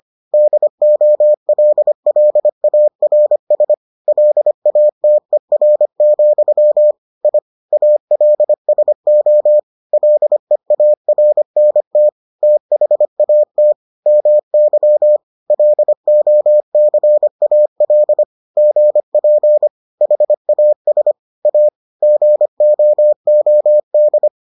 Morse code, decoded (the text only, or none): DOLLARS LATER, I ALSO LEARNT THAT MY LOCAL GP HAS A GOOD